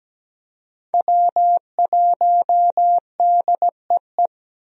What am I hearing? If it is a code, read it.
Morse code, heard as a keyed tone, about 17 wpm, 700 Hz. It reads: W1DEE
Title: USA Call Signs - 5 Characters - 17wpm